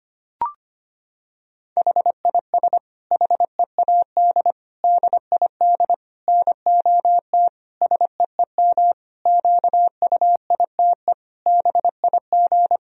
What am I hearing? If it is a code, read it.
Morse code, heard as a keyed tone, about 25 wpm, 700 Hz. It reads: HIS HEAD DID NOT SEEM QUITE BIG
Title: Sentences from Top 1000 Words - Review Part 2 - 25wpm